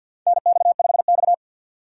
Morse code, heard as a keyed tone, about 51 wpm, 700 Hz. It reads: NX5X
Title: USA Call Signs - 4 Characters - 50wpm